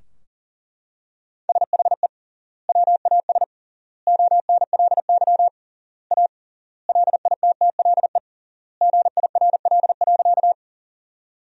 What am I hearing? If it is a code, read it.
Morse code, heard as a keyed tone, about 40 wpm, 700 Hz. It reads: SHE WAS ONLY A LITTLE GIRL.
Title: 3x Word-Spacing - Sentences from Top 400 Words - 40wpm